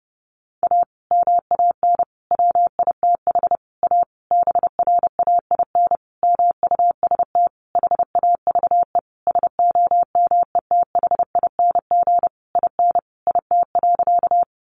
Morse code, heard as a keyed tone, about 30 wpm, 700 Hz. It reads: A MAN WITH A BRAIN MUST HAVE SOMETHING IN IT.
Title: Sentences from Top 1100 Words 30wpm